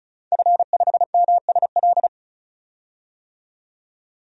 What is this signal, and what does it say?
Morse code, heard as a keyed tone, about 36 wpm, 700 Hz. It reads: F5MSL
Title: DXCC-100 Call Signs - 5 Characters - 35wpm